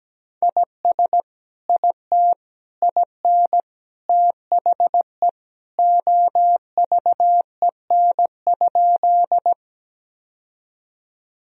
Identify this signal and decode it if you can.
Morse code, heard as a keyed tone, about 17 wpm, 700 Hz. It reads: IS IT IN THE OVEN?
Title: Sentences from Top 1000 Words - Review Part 4 - 17wpm